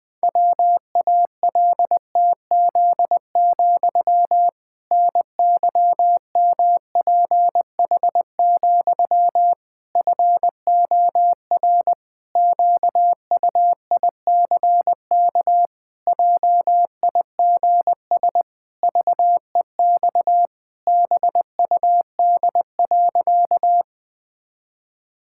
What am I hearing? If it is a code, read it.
Morse code, heard as a keyed tone, about 20 wpm, 700 Hz. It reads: WALTZ, NYMPH, FOR QUICK JIGS VEX BUD.
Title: Pangram Challenge v2 - 20wpm